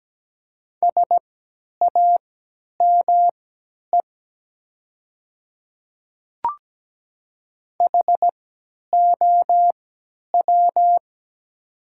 Morse code, heard as a keyed tone, about 17 wpm, spaced as for 10 wpm, 700 Hz. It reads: SAME HOW
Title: 3x Character Spacing - Top 500 Words - 17wpm